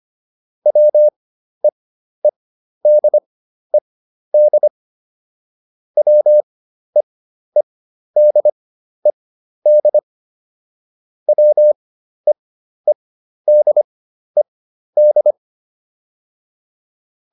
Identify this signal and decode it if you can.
Morse code, heard as a keyed tone, about 25 wpm, spaced as for 12 wpm, 600 Hz. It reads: WEEDED WEEDED WEEDED
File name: ED6 12